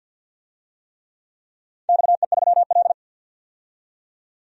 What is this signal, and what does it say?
Morse code, heard as a keyed tone, about 51 wpm, 700 Hz. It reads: XE3L
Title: DXCC-100 Call Signs - 4 Characters - 50wpm